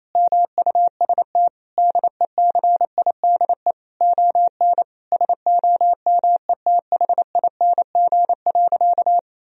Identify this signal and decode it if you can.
Morse code, heard as a keyed tone, about 27 wpm, 700 Hz. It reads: MUST DECIDE ON SOMETHING.
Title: Sentences from Top 1000 Words - Review Part 3 - 28wpm